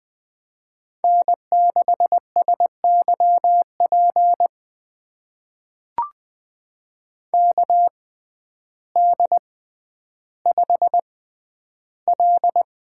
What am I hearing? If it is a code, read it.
Morse code, heard as a keyed tone, about 20 wpm, spaced as for 11 wpm, 700 Hz. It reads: N6SYP KD5L
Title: USA Call Signs - 5 Characters - 6x Character Spacing - 20wpm